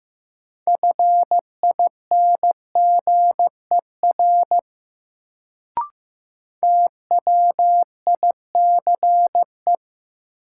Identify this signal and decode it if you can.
Morse code, heard as a keyed tone, about 15 wpm, 700 Hz. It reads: FINGER TWICE
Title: Sentences from Top 1600 Words - New Word Drill Alt - 15wpm